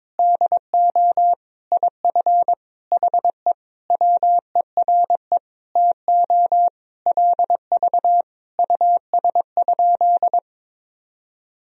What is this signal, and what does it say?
Morse code, heard as a keyed tone, about 22 wpm, 700 Hz. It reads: DO IF HE WERE TO LV US?
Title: Sentences from Top 400 Words and Abbreviations 22wpm